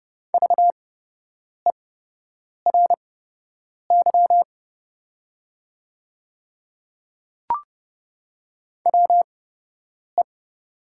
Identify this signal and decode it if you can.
Morse code, heard as a keyed tone, about 30 wpm, spaced as for 8 wpm, 700 Hz. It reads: VERY WE
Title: 8x Character Spacing - Top 300 Words - 30wpm